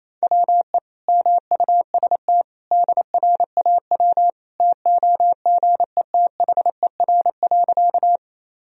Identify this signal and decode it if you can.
Morse code, heard as a keyed tone, about 28 wpm, 700 Hz. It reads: WE MUST DRAW TOGETHER.
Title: Sentences from Top 1000 Words - Review Part 1 - 28wpm